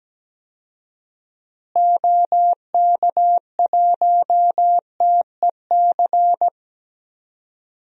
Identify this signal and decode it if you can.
Morse code, heard as a keyed tone, about 17 wpm, 700 Hz. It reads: OK1TEC
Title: DXCC-100 Call Signs - 6 Characters - Repeated 3x - 17wpm